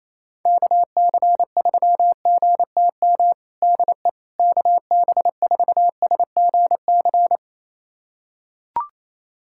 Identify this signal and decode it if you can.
Morse code, heard as a keyed tone, about 28 wpm, 700 Hz. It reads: KC3GTM DE KB4SGC
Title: USA Callsign DE Callsign - 6 Characters - 28wpm